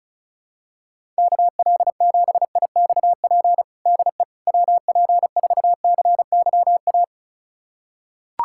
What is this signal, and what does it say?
Morse code, heard as a keyed tone, about 36 wpm, 700 Hz. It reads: KL7IXP DE WP4CYA
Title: USA Callsign DE Callsign - 6 Characters - 35wpm